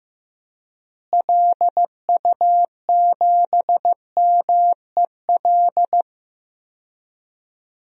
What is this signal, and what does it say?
Morse code, heard as a keyed tone, about 15 wpm, 700 Hz. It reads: LU7MEL
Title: DXCC-100 Call Signs - 6 Characters - Repeated 3x - 15wpm